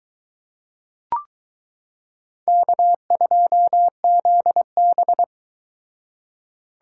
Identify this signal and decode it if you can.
Morse code, heard as a keyed tone, about 22 wpm, 700 Hz. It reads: K2ZB